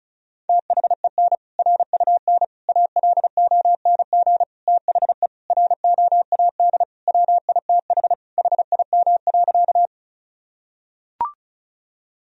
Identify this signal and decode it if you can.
Morse code, heard as a keyed tone, about 36 wpm, 700 Hz. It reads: THEN RUN ALONG THE ROAD WITH HIM.
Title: Sentences from Top 1000 Words - Review Part 2 - 35wpm